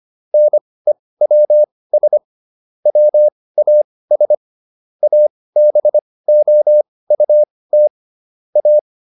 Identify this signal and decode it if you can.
Morse code, heard as a keyed tone, about 25 wpm, spaced as for 18 wpm, 600 Hz. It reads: NEWS WAS ABOUT A